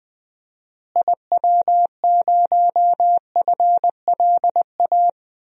Morse code, heard as a keyed tone, about 21 wpm, 700 Hz. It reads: IW0FLA
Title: DXCC-100 Call Signs - 6 Characters - Repeated 3x - 20wpm